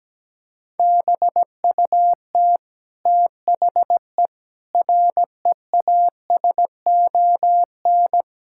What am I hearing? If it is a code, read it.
Morse code, heard as a keyed tone, about 17 wpm, 700 Hz. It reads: BUT THE REASON
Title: Sentences from Top 1600 Words 17wpm